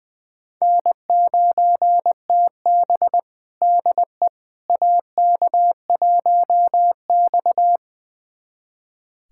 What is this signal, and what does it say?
Morse code, heard as a keyed tone, about 21 wpm, 700 Hz. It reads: N9TB DE AK1X
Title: USA Callsign DE Callsign - 4 Characters - 20wpm